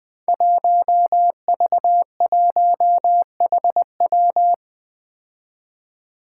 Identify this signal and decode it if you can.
Morse code, heard as a keyed tone, about 20 wpm, 700 Hz. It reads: 1V1HW
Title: Five Challenging Letter-Number Combinations 20wpm